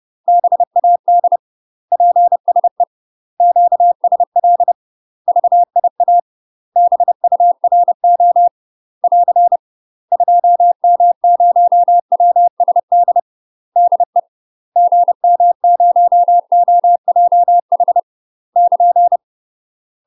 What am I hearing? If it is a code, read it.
Morse code, heard as a keyed tone, about 30 wpm, 700 Hz. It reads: BAD PSE QSL VIA BURO <AR> 2M0WSD DE GM0OJH <KN>
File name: UK MASTER QSOs TRAINING / 30WPM / UK MASTER QSOs 0048